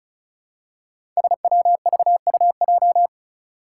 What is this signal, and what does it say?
Morse code, heard as a keyed tone, about 36 wpm, 700 Hz. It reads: SWVUJ
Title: Five Challenging Letter-Number Combinations 35wpm